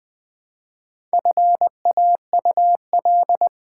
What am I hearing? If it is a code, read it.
Morse code, heard as a keyed tone, about 20 wpm, 700 Hz. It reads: FAUL
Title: Build A Word - 1 Letter At A Time - 5-Letter Words - 20wpm